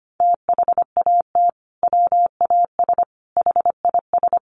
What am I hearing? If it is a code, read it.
Morse code, heard as a keyed tone, about 25 wpm, 700 Hz. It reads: THAT WAS HIS
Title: Sentences from Top 1500 Words 25wpm